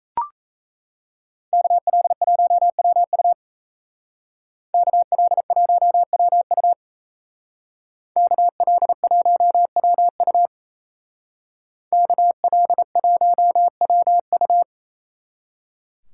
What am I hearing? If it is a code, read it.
Morse code, heard as a keyed tone, about 36 wpm, 700 Hz. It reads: KL1WU KL1WU KL1WU KL1WU